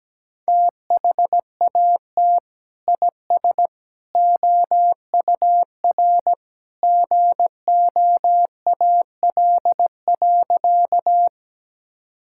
Morse code, heard as a keyed tone, about 17 wpm, 700 Hz. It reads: THAT IS OUR GOAL.
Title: Sentences from Top 1000 Words - Review Part 5 - 17wpm